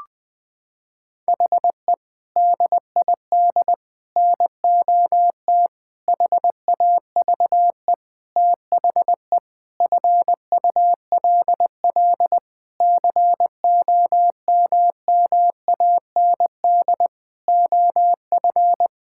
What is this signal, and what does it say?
Morse code, heard as a keyed tone, about 20 wpm, 700 Hz. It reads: HE DID NOT HAVE THE FULL COMMAND OF
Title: Sentences from Top 1600 Words 20wpm